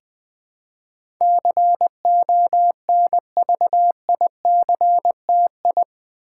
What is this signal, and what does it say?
Morse code, heard as a keyed tone, about 20 wpm, 700 Hz. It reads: CONVICTI
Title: Build A Word - 4 Letters At A Time - 10-Letter Words - 20wpm